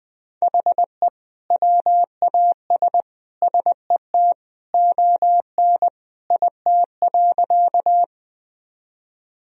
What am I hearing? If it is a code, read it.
Morse code, heard as a keyed tone, about 20 wpm, 700 Hz. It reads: HE WAS SET ON IT.
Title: Sentences from Top 1000 Words - Review Part 1 - 20wpm